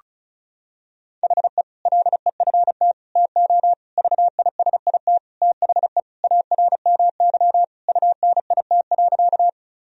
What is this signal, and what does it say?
Morse code, heard as a keyed tone, about 36 wpm, 700 Hz. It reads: HE LEFT TO VISIT THE ARMY UNIT.
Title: Sentences from Top 1000 Words - Review Part 2 - 35wpm